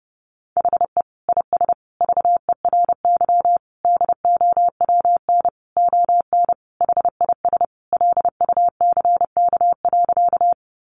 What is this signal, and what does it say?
Morse code, heard as a keyed tone, about 30 wpm, 700 Hz. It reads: HE IS VERY DOWN ON HIS LUCK.
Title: Sentences from Top 1500 Words 30wpm